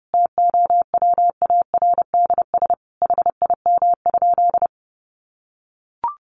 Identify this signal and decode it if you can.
Morse code, heard as a keyed tone, about 30 wpm, 700 Hz. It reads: TOWARDS HIM?
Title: Sentences from Top 1100 Words 30wpm